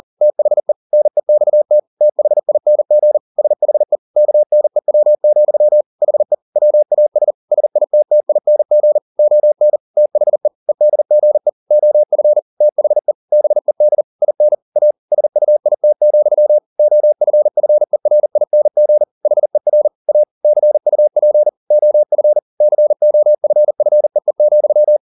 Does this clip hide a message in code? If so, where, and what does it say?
Yes: Morse code from 0.2 to 25.0 s, heard as a keyed tone, about 40 wpm, 600 Hz. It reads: THE NEXT THING SHE KNEW, HE WAS SITTING ON THE EDGE OF THE BED IN A SUIT, OFFERING HER A CUP OF COFFEE,